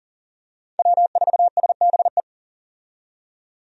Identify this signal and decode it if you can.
Morse code, heard as a keyed tone, about 40 wpm, 700 Hz. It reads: W4SBE
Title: USA Call Signs - 5 Characters - 40wpm